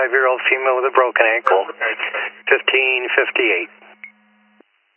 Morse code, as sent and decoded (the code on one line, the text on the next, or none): none